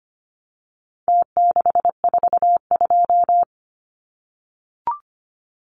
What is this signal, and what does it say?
Morse code, heard as a keyed tone, about 25 wpm, 700 Hz. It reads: T642
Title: Numbers - Contest Style - 4-Digit 25wpm